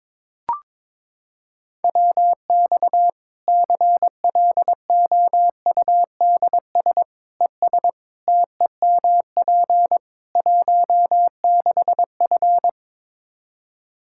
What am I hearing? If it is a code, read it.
Morse code, heard as a keyed tone, about 22 wpm, 700 Hz. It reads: WX CLOUDS ES TEMP 16F
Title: Practice QSO - Exchange 2 - Info - 22wpm